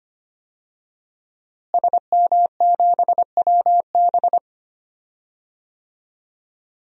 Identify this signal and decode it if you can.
Morse code, heard as a keyed tone, about 25 wpm, 700 Hz. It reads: SM7WB
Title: DXCC-100 Call Signs - 5 Characters - Repeated 2x - 25wpm